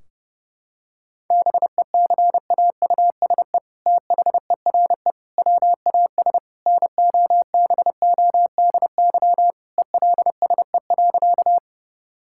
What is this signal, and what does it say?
Morse code, heard as a keyed tone, about 30 wpm, 700 Hz. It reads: BECAUSE THERE WAS NOBODY ELSE.
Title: Sentences from Top 1600 Words 30wpm